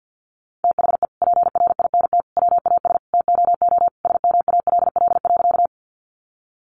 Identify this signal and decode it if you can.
Morse code, heard as a keyed tone, about 50 wpm, 700 Hz. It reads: THE PRINT WAS TOO SMALL.
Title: Sentences from Top 1400 Words 50wpm